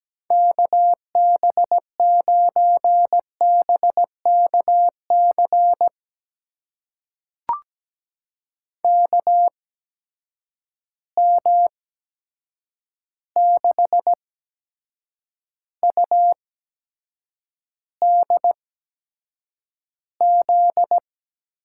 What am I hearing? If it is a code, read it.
Morse code, heard as a keyed tone, about 17 wpm, spaced as for 7 wpm, 700 Hz. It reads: KB9BKC KM6UDZ
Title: USA Call Signs - 6 Characters - 8x Character Spacing - 17wpm